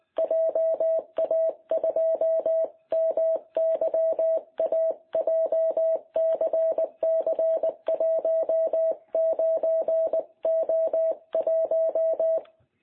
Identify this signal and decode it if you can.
Morse code, heard as a keyed tone, about 20 wpm, 660 Hz. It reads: JA2MYA JCC19O1
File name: jh8cbh / 2021cw